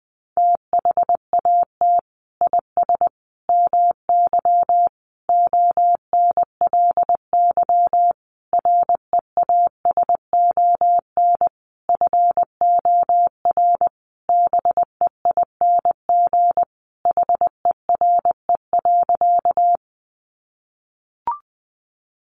Morse code, heard as a keyed tone, about 20 wpm, 700 Hz. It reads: THAT IS MY ONLY REASON FOR BEING HERE.
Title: Sentences from Top 700 Words 20wpm